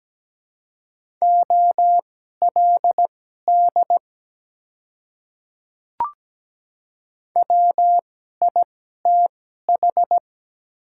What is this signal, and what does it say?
Morse code, heard as a keyed tone, about 17 wpm, spaced as for 12 wpm, 700 Hz. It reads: OLD WITH
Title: 2x Character Spacing - Top 200 Words - 17wpm